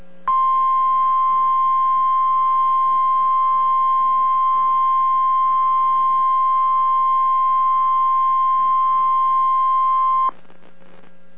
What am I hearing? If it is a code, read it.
Emergency Alert System (SAME) tones (the 1050 Hz attention tone). It is heard from 0.2 to 10.3 s.